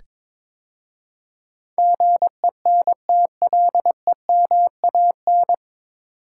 Morse code, heard as a keyed tone, about 21 wpm, 700 Hz. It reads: GENTLEMAN
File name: 9 Letter Words - First 3 Letters Given - 22wpm